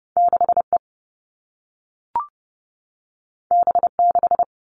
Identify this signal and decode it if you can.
Morse code, heard as a keyed tone, about 30 wpm, 700 Hz. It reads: 6E B6
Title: All Cut Numbers - 2-Digit 30wpm